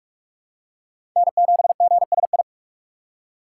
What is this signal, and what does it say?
Morse code, heard as a keyed tone, about 46 wpm, 700 Hz. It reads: N7GSI
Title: USA Call Signs - 5 Characters - 3x Character Spacing - 45wpm